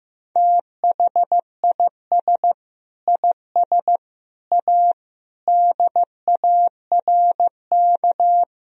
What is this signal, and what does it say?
Morse code, heard as a keyed tone, about 15 wpm, 700 Hz. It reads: THIS IS A DARK
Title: Sentences from Top 1000 Words - Review Part 1 - 15wpm